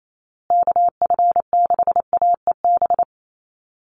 Morse code, heard as a keyed tone, about 28 wpm, 700 Hz. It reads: KF6AEB
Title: USA Call Signs Repeated 3x 28wpm